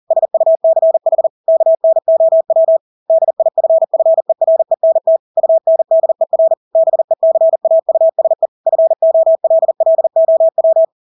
Morse code, heard as a keyed tone, about 40 wpm, 650 Hz. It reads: SUCH KNOW DIFFERENT UNDER BECAUSE FOLLOW
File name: Engelske ord 40-40 wpm